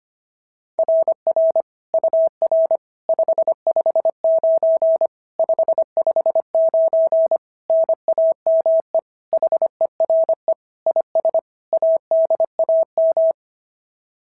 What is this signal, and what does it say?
Morse code, heard as a keyed tone, about 25 wpm, 690 Hz. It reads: RR UR 559 559 NAME HERE IS ADAM
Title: SOTA QSO Practice 25wpm